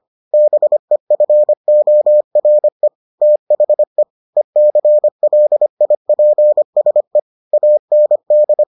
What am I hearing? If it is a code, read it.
Morse code, heard as a keyed tone, about 25 wpm, 600 Hz. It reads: BEFORE THE ECLIPSE AND